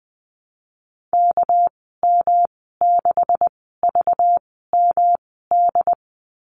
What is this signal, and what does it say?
Morse code, heard as a keyed tone, about 20 wpm, spaced as for 15 wpm, 700 Hz. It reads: KM6VMD
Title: 200 Lesson 32g - Letter k - Call Signs 20wpm